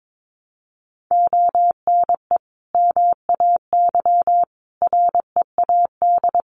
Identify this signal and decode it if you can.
Morse code, heard as a keyed tone, about 22 wpm, 700 Hz. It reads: ONE MAY READ
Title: Sentences from Top 1100 Words 22wpm